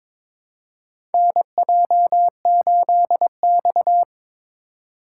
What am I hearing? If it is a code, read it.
Morse code, heard as a keyed tone, about 22 wpm, 700 Hz. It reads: NJ8X
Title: USA Call Signs - 4x Character Spacing - 22wpm